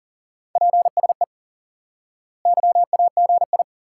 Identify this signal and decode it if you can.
Morse code, heard as a keyed tone, about 40 wpm, 700 Hz. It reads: PSE YAGI
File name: IQE Course 3 - Lesson 89 - YAGI - Two Elements - 40wpm